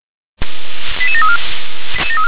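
Five-tone selective call (ZVEI), heard as quick runs of five tones, tones of about 70 ms. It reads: 88734
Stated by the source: ZVEI five-tone selective call received on an SDR.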